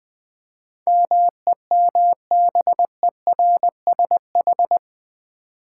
Morse code, heard as a keyed tone, about 19 wpm, 700 Hz. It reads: MEMBERSH
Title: Build A Word - 4 Letters At A Time - 10-Letter Words - 20wpm